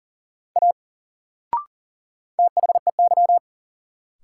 Morse code, heard as a keyed tone, about 40 wpm, 700 Hz. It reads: A THEY
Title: Sentences from Top 100 Words - New Word Drill - 40wpm